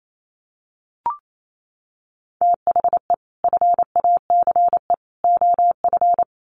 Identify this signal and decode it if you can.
Morse code, heard as a keyed tone, about 28 wpm, 700 Hz. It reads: THE FACE OF
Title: Sets of 3 Words 28wpm